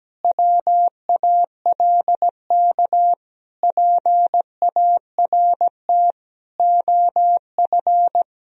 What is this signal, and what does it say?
Morse code, heard as a keyed tone, about 17 wpm, 700 Hz. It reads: WALK PART OF
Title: Sentences from Top 1000 Words - Review Part 1 - 17wpm